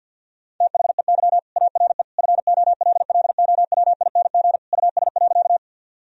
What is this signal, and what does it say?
Morse code, heard as a keyed tone, about 50 wpm, 700 Hz. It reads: THEY ARE FOLLOWING US.
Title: Sentences from Top 1000 Words - Review Part 4 - 50wpm